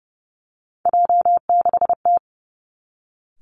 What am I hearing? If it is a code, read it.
Morse code, heard as a keyed tone, about 30 wpm, 700 Hz. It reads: J6T
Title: Three Challenging Letter-Number Combinations 30wpm